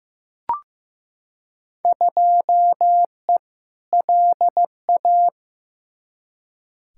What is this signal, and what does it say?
Morse code, heard as a keyed tone, about 15 wpm, 700 Hz. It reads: 2E LA
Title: Field Day Exchanges Encore - 15wpm